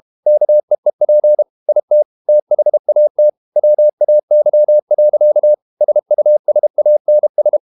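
Morse code, heard as a keyed tone, about 32 wpm, 600 Hz. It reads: KEEP IT THAT WAY. SUSANS
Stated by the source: Morse code practice at 32 wpm from CW med Gnister, reading Ken Follett's novel Never in English.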